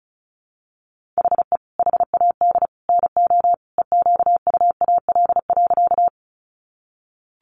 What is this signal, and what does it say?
Morse code, heard as a keyed tone, about 34 wpm, 700 Hz. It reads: HE HAD NO EQUAL.
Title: Sentences from Top 1400 Words 35wpm